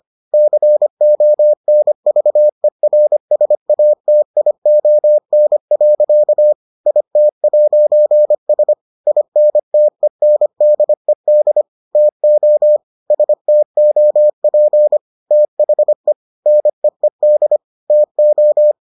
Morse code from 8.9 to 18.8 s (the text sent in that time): INTENDED TO STOP THE NEED TO